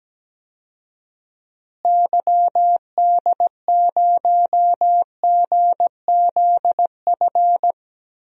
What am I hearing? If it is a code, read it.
Morse code, heard as a keyed tone, about 17 wpm, 700 Hz. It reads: YD0GZF
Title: DXCC-100 Call Signs - 6 Characters - Repeated 3x - 17wpm